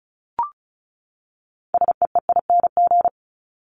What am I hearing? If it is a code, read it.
Morse code, heard as a keyed tone, about 34 wpm, 700 Hz. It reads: SEEING
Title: ING Suffix - Top 100 Words with Base Words 35wpm